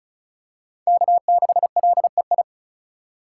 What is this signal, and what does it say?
Morse code, heard as a keyed tone, about 36 wpm, 700 Hz. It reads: K6LEI
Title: USA Call Signs - 5 Characters - 4x Character Spacing - 35wpm